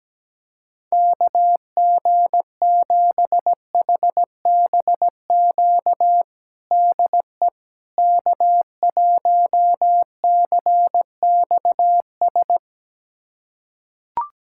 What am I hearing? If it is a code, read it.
Morse code, heard as a keyed tone, about 17 wpm, 700 Hz. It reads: KG7HBQ DE K1CXS
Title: USA Callsign DE Callsign 17wpm